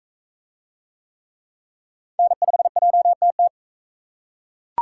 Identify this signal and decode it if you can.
Morse code, heard as a keyed tone, about 40 wpm, 700 Hz. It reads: N5JTT